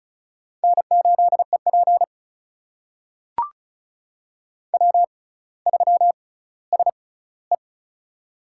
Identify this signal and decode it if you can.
Morse code, heard as a keyed tone, about 36 wpm, spaced as for 18 wpm, 700 Hz. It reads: N8EP W3SE
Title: USA Call Signs - 4 Characters - 6x Character Spacing - 35wpm